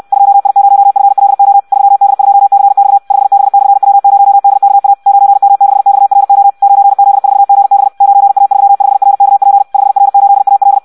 Morse code, heard as a keyed tone, about 64 wpm, 800 Hz. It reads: 2I0FLO 3D2CX 5B/W0DKA 9M6XRO 9Z4GB 9A35RKP 5R8AL
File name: Audio practica CW / qrq-tx-66 2